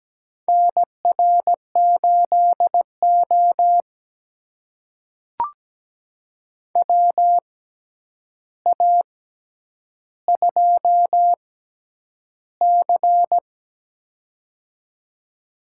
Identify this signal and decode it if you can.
Morse code, heard as a keyed tone, about 17 wpm, spaced as for 9 wpm, 700 Hz. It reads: NR8O WA2C
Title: USA Call Signs - 4 Characters - 6x Character Spacing - 17wpm